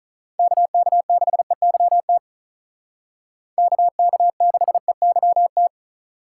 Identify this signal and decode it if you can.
Morse code, heard as a keyed tone, about 43 wpm, 700 Hz. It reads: KK6EYT KK6EYT
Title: Speed Racing - USA 2x3 Call Signs 35wpm